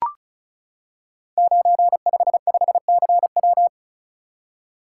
Morse code, heard as a keyed tone, about 36 wpm, 690 Hz. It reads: 955CW
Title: Code Groups - 5 Letter:Numbers - 35wpm